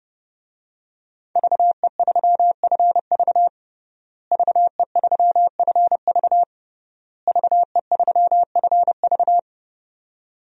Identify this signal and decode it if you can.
Morse code, heard as a keyed tone, about 30 wpm, 700 Hz. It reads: VE3FV VE3FV VE3FV
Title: DXCC-100 Call Signs - 5 Characters - Repeated 3x - 30wpm